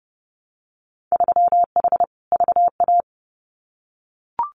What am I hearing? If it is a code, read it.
Morse code, heard as a keyed tone, about 30 wpm, 700 Hz. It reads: 3H VA
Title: Winter Field Day Exchanges 30wpm